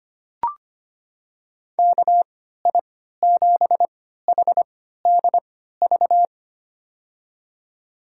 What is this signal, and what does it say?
Morse code, heard as a keyed tone, about 25 wpm, spaced as for 14 wpm, 700 Hz. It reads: KI7HDV
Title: USA Call Signs - 6 Characters - 3x Character Spacing - 25wpm